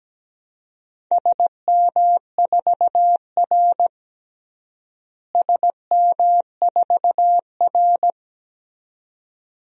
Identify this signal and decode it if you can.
Morse code, heard as a keyed tone, about 17 wpm, 700 Hz. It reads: SM4R SM4R
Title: DXCC-100 Call Signs - 4 Characters - Repeated 2x - 17wpm